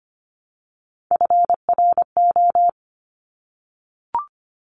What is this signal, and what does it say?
Morse code, heard as a keyed tone, about 25 wpm, 700 Hz. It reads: FRO
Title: Top Three Letter Combinations in Words 25wpm